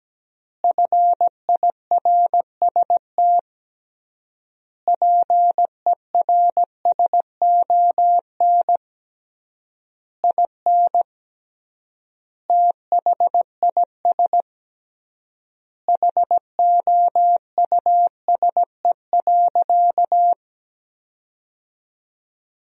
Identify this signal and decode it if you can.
Morse code, heard as a keyed tone, about 17 wpm, 700 Hz. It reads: FIRST PERSON IN THIS HOUSE.
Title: 3x Word-Spacing - Sentences from Top 400 Words - 17wpm